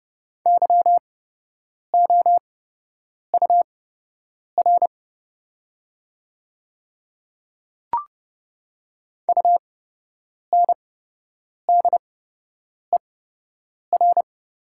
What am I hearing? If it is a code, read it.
Morse code, heard as a keyed tone, about 30 wpm, spaced as for 8 wpm, 690 Hz. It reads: YOUR UNDER
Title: 8x Character Spacing - Top 200 Words - 30wpm